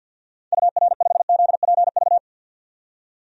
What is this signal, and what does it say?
Morse code, heard as a keyed tone, about 50 wpm, 700 Hz. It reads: UR5ZPV
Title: DXCC-100 Call Signs - 6 Characters - 50wpm